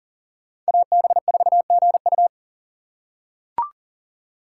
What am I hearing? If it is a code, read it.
Morse code, heard as a keyed tone, about 40 wpm, 700 Hz. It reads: AB4GU
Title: USA Call Signs - 5 Characters - 2x Character Spacing - 40wpm